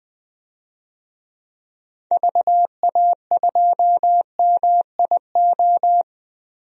Morse code, heard as a keyed tone, about 20 wpm, 700 Hz. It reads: VA2MIO
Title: DXCC-100 Call Signs - 6 Characters - 20wpm